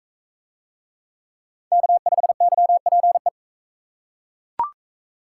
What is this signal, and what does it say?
Morse code, heard as a keyed tone, about 39 wpm, 700 Hz. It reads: K5YPE